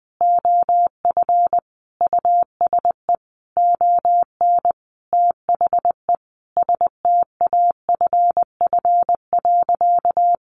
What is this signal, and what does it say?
Morse code, heard as a keyed tone, about 20 wpm, 700 Hz. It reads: OF USE ON THE STAFF.
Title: Sentences from Top 1200 Words 20wpm